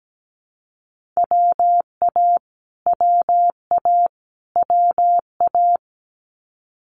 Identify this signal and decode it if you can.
Morse code, heard as a keyed tone, about 17 wpm, 700 Hz. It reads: WA WA WA
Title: Multi-letter Phonemes Repeated 3x 17wpm